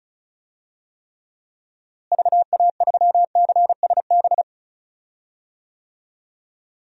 Morse code, heard as a keyed tone, about 34 wpm, 700 Hz. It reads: VA3CSB
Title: DXCC-100 Call Signs - 6 Characters - Repeated 3x - 35wpm